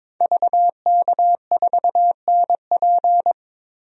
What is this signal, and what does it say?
Morse code, heard as a keyed tone, about 22 wpm, 700 Hz. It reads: VK4NP
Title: DXCC-100 Call Signs - 5 Characters - Repeated 3x - 22wpm